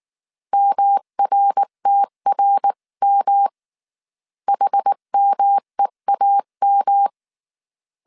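Morse code, heard as a keyed tone, about 19 wpm, 800 Hz. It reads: MRTRM HMEAM
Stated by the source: Morse code dictation at 18 wpm from the IW3SGT QRS Morse code training course, five-character groups of 300 characters.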